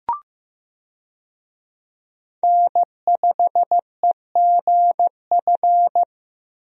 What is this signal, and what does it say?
Morse code, heard as a keyed tone, about 15 wpm, 700 Hz. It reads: N5EGF
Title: USA Call Signs - 5 Characters - Repeated 2x - 15wpm